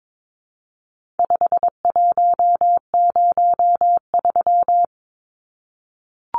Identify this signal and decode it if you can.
Morse code, heard as a keyed tone, about 22 wpm, 700 Hz. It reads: 5103
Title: Numbers - 4-Digit 22wpm